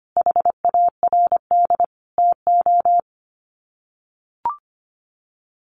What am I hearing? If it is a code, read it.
Morse code, heard as a keyed tone, about 25 wpm, 700 Hz. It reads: HARD TO
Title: Sets of 5 Words Encore2 25wpm